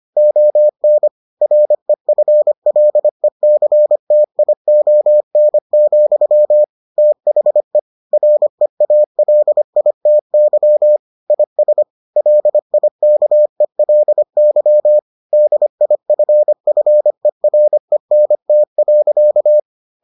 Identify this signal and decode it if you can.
Morse code, heard as a keyed tone, about 25 wpm, 600 Hz. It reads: ON REFLECTION, THE REALITY IS LIKELY DIFFERENT.